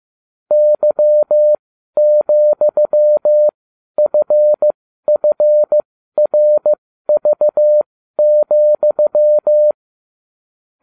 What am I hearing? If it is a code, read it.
Morse code, heard as a keyed tone, about 15 wpm, spaced as for 12 wpm, 600 Hz. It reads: Y,FFRV,